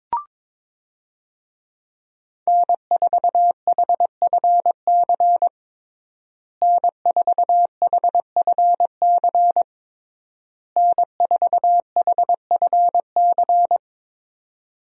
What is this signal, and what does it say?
Morse code, heard as a keyed tone, about 22 wpm, 700 Hz. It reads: N4HFC N4HFC N4HFC
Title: USA Call Signs - 5 Characters - Repeated 3x - 22wpm